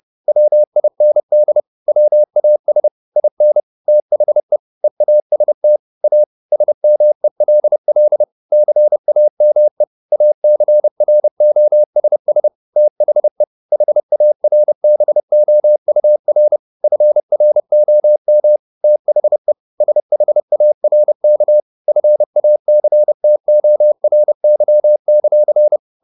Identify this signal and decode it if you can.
Morse code, heard as a keyed tone, about 30 wpm, 600 Hz. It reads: WIND WAS IN THE EAST A SMELL CAME ACROSS THE HARBOUR FROM THE SHARK FACTORY;
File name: the old man and the sea / the old man and the sea 1 30wpm